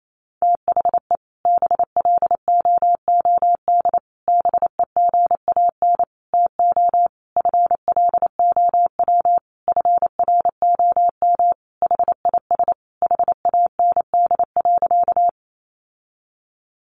Morse code, heard as a keyed tone, about 28 wpm, 700 Hz. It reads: THE BLOOD BEGAN TO FLOW FROM HIS HAND.
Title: Sentences from Top 1200 Words 28wpm